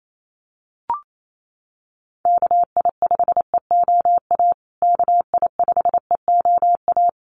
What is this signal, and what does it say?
Morse code, heard as a keyed tone, about 28 wpm, 700 Hz. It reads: KI5EOA KI5EOA
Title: USA Call Signs Repeated 3x 28wpm